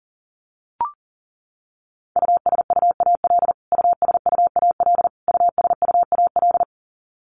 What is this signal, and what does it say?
Morse code, heard as a keyed tone, about 40 wpm, 700 Hz. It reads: USUAL USUAL USUAL
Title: Top 500 Words Repeated 40wpm